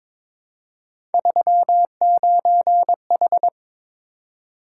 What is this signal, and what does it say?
Morse code, heard as a keyed tone, about 22 wpm, 700 Hz. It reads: 39H
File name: Three Letter-Number Combinations - 22wpm